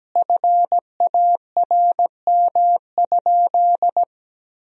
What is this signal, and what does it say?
Morse code, heard as a keyed tone, about 17 wpm, 700 Hz. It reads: FARM?
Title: Sentences from Top 1000 Words - Review Part 2 - 17wpm